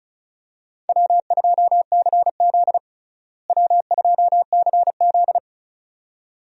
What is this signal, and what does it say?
Morse code, heard as a keyed tone, about 34 wpm, 700 Hz. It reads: W2CZ W2CZ
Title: USA Call Signs - 4 Characters - Repeated 2x - 35wpm